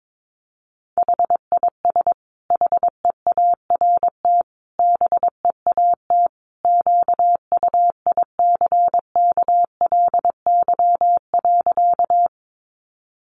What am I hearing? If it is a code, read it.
Morse code, heard as a keyed tone, about 22 wpm, 700 Hz. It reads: HIS HEART BEAT QUICKLY.
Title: Sentences from Top 1300 Words 22wpm